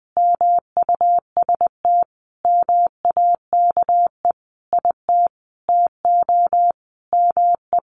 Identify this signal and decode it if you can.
Morse code, heard as a keyed tone, about 20 wpm, 700 Hz. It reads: MUST MAKE IT TO ME
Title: Sentences from Top 500 Words 20wpm